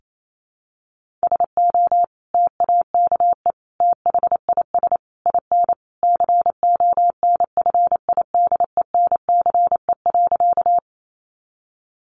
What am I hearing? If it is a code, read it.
Morse code, heard as a keyed tone, about 28 wpm, 700 Hz. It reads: SO TAKE THIS IN CONFIDENCE.
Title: Sentences from Top 1500 Words 28wpm